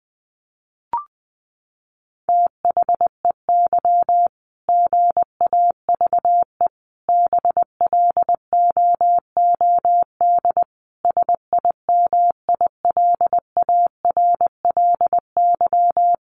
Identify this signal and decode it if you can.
Morse code, heard as a keyed tone, about 20 wpm, 700 Hz. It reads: THEY GAVE BLOOD SIMILARLY
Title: Sentences from Top 1400 Words 20wpm